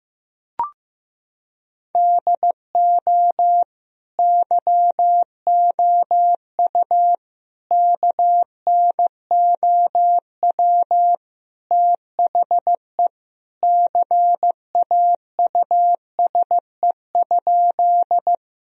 Morse code, heard as a keyed tone, about 15 wpm, 700 Hz. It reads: DO YOU KNOW THE CAUSE?
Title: Sentences from Top 1000 Words - Review Part 1 - 15wpm